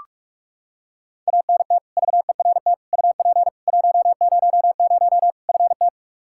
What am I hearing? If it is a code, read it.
Morse code, heard as a keyed tone, about 44 wpm, 700 Hz. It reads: ANT VERT UP 100 FT
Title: Practice QSO - Exchange 2 - Info - 45wpm